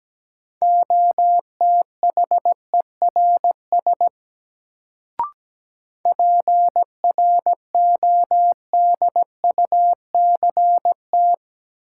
Morse code, heard as a keyed tone, about 17 wpm, 700 Hz. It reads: OTHERS PRODUCT
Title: Sentences from Top 1000 Words - Review Part 2 - New Word Drill - 17wpm